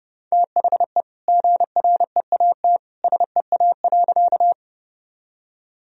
Morse code, heard as a keyed tone, about 30 wpm, 700 Hz. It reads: THE GREAT SEA.
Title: Sentences from Top 1000 Words - Review Part 1 - 30wpm